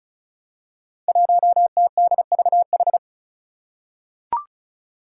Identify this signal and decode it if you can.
Morse code, heard as a keyed tone, about 34 wpm, 700 Hz. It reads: 1TDVH
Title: Five Challenging Letter-Number Combinations 35wpm